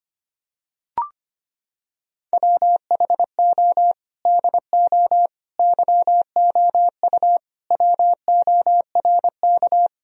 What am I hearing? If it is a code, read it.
Morse code, heard as a keyed tone, about 25 wpm, 700 Hz. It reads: WHO DO YOU WORK
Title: Sentences from Top 1000 Words - Review Part 4 - 25wpm